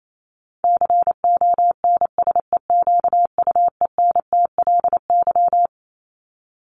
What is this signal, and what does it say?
Morse code, heard as a keyed tone, about 28 wpm, 700 Hz. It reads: CONSEQUENTLY
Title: LY Suffix - Top 100 Words with Base Words 28wpm